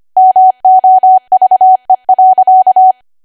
Morse code, heard as a keyed tone, about 25 wpm, 750 Hz. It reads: MOVE.
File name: Archive / 25 WPM / 25WPM